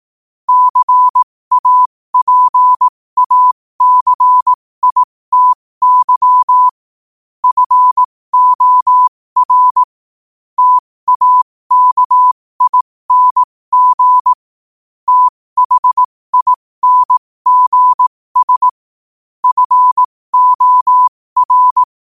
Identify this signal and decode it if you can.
Morse code, heard as a keyed tone, about 18 wpm, spaced as for 15 wpm, 1000 Hz. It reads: CAPACITY FOR TAKING THINGS FOR
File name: qotdfiles / 15wpm